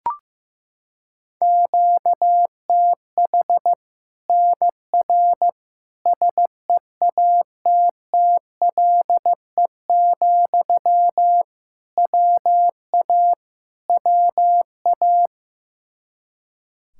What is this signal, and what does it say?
Morse code, heard as a keyed tone, about 16 wpm, 700 Hz. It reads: QTH NR SEATTLE, WA WA
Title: Practice QSO - Exchange 1 - Beginning - 15wpm